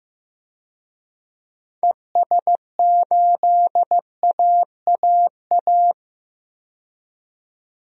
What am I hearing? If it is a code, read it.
Morse code, heard as a keyed tone, about 15 wpm, 700 Hz. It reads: ES8AAA
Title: DXCC-100 Call Signs - 6 Characters - 15wpm